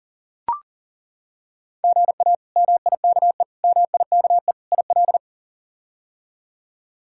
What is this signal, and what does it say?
Morse code, heard as a keyed tone, about 40 wpm, 700 Hz. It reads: GA MIKE MIKE IL
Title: K1USN SST QSO Practice - 40wpm